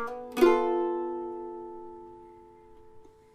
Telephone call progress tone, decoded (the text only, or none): dial tone